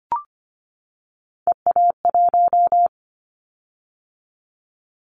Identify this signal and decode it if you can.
Morse code, heard as a keyed tone, about 25 wpm, 700 Hz. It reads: EA1
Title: Callsign Prefixes for Top 25 DXCC - 25wpm